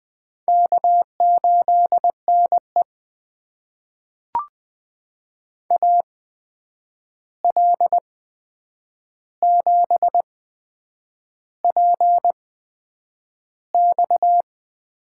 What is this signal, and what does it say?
Morse code, heard as a keyed tone, about 20 wpm, spaced as for 8 wpm, 700 Hz. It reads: K8NE AL7PX
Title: USA Call Signs - 8x Character Spacing - 20wpm